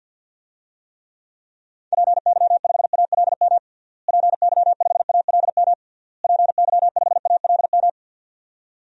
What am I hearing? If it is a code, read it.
Morse code, heard as a keyed tone, about 53 wpm, 700 Hz. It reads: PY5ALM PY5ALM PY5ALM
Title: DXCC-100 Call Signs - 6 Characters - Repeated 3x - 50wpm